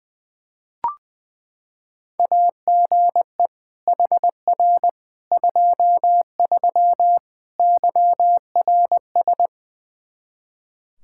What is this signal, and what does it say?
Morse code, heard as a keyed tone, about 21 wpm, 700 Hz. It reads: AGE HR 23 YRS
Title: K1USN SST QSO Practice 20wpm